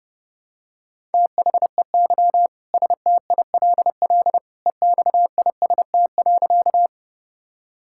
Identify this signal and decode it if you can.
Morse code, heard as a keyed tone, about 32 wpm, 700 Hz. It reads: THEY STILL EXIST.
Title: Sentences from Top 1000 Words - Review Part 5 - 30wpm